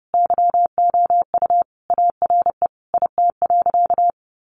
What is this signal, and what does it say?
Morse code, heard as a keyed tone, about 30 wpm, 700 Hz. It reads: YOU ARE IT.
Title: Additional Easy Phrases 30wpm